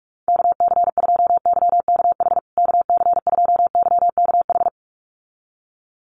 Morse code, heard as a keyed tone, about 44 wpm, 700 Hz. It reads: KC2YKH KC2YKH
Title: USA Call Signs Repeated 2x 45wpm